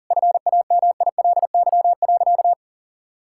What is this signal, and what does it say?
Morse code, heard as a keyed tone, about 40 wpm, 700 Hz. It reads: FAMILY.
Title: Sentences from Top 1000 Words - Review Part 5 - 40wpm